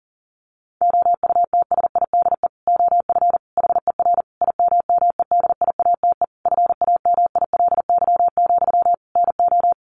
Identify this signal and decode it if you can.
Morse code, heard as a keyed tone, about 40 wpm, 700 Hz. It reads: OUTSIDE OF HER IMMEDIATE FAMILY, NO
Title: Sentences from Top 1200 Words 40wpm